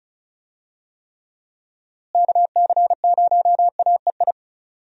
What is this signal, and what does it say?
Morse code, heard as a keyed tone, about 34 wpm, 700 Hz. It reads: KC0AEI
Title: USA Call Signs - 6 Characters - Repeated 2x - 35wpm